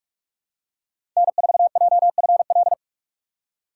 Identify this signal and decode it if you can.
Morse code, heard as a keyed tone, about 46 wpm, 700 Hz. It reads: N4JFL
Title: USA Call Signs - 5 Characters - 2x Character Spacing - 45wpm